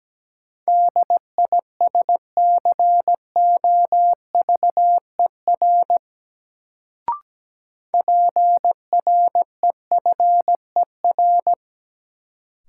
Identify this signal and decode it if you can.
Morse code, heard as a keyed tone, about 17 wpm, 700 Hz. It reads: DISCOVER PREFER
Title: Sentences from Top 1100 Words - New Word Drill - 17wpm